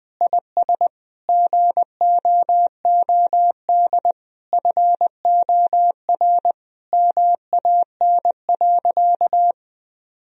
Morse code, heard as a keyed tone, about 20 wpm, 700 Hz. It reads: IS GOOD FOR MAN.
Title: Sentences from Top 1000 Words - Review Part 3 - 20wpm